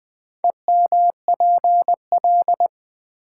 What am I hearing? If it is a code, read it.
Morse code, heard as a keyed tone, about 20 wpm, 700 Hz. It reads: EMPL
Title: Build A Word - 2 Letters At A Time - 8-Letter Words - 20wpm